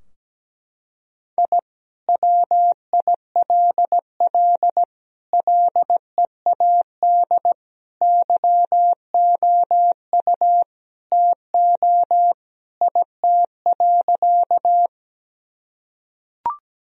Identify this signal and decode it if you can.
Morse code, heard as a keyed tone, about 17 wpm, 700 Hz. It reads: I WILL LEAD YOU TO IT.
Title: Sentences from Top 1000 Words - Review Part 2 - 17wpm